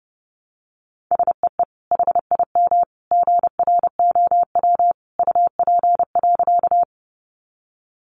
Morse code, heard as a keyed tone, about 30 wpm, 700 Hz. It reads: SEE HIM GROW UP.
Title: Sentences from Top 300 Words 30wpm